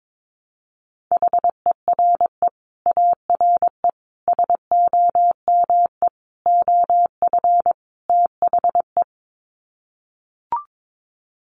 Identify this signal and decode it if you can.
Morse code, heard as a keyed tone, about 22 wpm, 700 Hz. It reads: HERE ARE SOME OF THE
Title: Sets of 5 Words Encore 22wpm